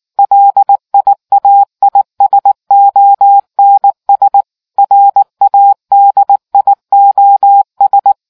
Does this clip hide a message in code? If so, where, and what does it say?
Yes: Morse code from 0.2 to 8.1 s, heard as a keyed tone, about 19 wpm, 800 Hz. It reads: LIAISONS RADIOS